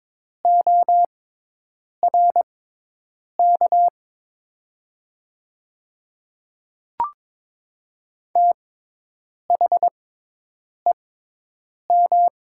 Morse code, heard as a keyed tone, about 22 wpm, spaced as for 8 wpm, 700 Hz. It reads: ORK THEM
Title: 6x Character Spacing - Top 300 Words - 22wpm